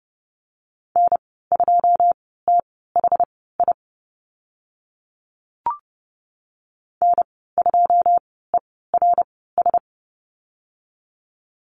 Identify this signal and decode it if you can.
Morse code, heard as a keyed tone, about 30 wpm, spaced as for 17 wpm, 700 Hz. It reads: N2THI N2ERS
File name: Lesson 15g - Number 5 - Call Signs 30wpm